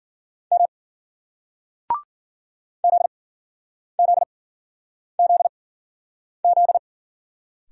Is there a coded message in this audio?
Yes: Morse code from 0.5 to 6.8 s, heard as a keyed tone, about 58 wpm, spaced as for 9 wpm, 700 Hz. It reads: M ZZZZ